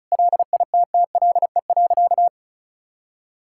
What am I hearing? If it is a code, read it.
Morse code, heard as a keyed tone, about 34 wpm, 700 Hz. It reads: LITTLE.
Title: Sentences from Top 1000 Words - Review Part 5 - 35wpm